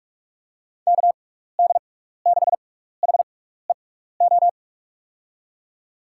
Morse code, heard as a keyed tone, about 44 wpm, spaced as for 15 wpm, 700 Hz. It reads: KD6HEO
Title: USA Call Signs - 6 Characters - 6x Character Spacing - 45wpm